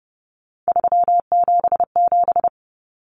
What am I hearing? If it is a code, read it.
Morse code, heard as a keyed tone, about 30 wpm, 700 Hz. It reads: 377
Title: Numbers - 3-Digit 30wpm